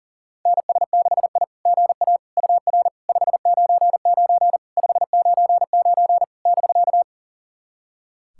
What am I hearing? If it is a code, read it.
Morse code, heard as a keyed tone, about 40 wpm, 700 Hz. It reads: NS6I GA UR 599 599 <BK>
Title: SOTA QSO Practice 40wpm